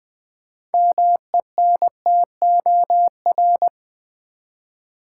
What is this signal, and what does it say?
Morse code, heard as a keyed tone, about 20 wpm, 700 Hz. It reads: MENTOR
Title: 6 Letter Words - First 3 Letters Given - 20wpm